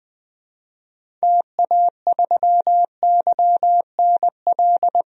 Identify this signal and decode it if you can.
Morse code, heard as a keyed tone, about 20 wpm, 700 Hz. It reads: TA3YNL
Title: DXCC-100 Call Signs - 6 Characters - Repeated 2x - 20wpm